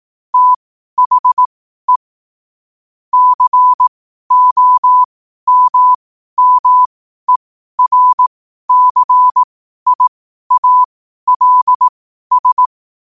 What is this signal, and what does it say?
Morse code, heard as a keyed tone, about 18 wpm, spaced as for 12 wpm, 1000 Hz. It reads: THE COMMERCIALS
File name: qotdfiles / 12wpm